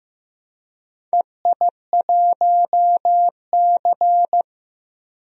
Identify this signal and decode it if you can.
Morse code, heard as a keyed tone, about 16 wpm, 700 Hz. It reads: EI1C
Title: DXCC-100 Call Signs - 4 Characters - Repeated 2x - 15wpm